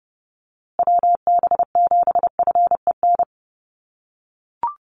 Morse code, heard as a keyed tone, about 28 wpm, 700 Hz. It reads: WB7FEN
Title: USA Call Signs Repeated 3x 30wpm